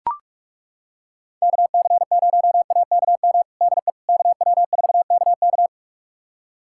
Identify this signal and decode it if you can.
Morse code, heard as a keyed tone, about 41 wpm, 700 Hz. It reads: KC0AKM DE KW4KK
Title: USA Callsign DE Callsign 45wpm